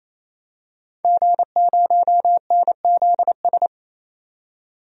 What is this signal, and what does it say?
Morse code, heard as a keyed tone, about 28 wpm, 700 Hz. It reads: G0NZS
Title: DXCC-100 Call Signs - 5 Characters - 28wpm